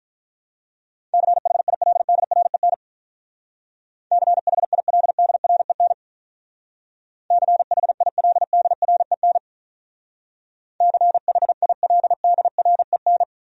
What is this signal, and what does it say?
Morse code, heard as a keyed tone, about 45 wpm, 700 Hz. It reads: CHILDREN CHILDREN CHILDREN CHILDREN